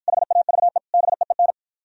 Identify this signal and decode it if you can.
Morse code, heard as a keyed tone, about 51 wpm, 700 Hz. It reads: HAVE BEEN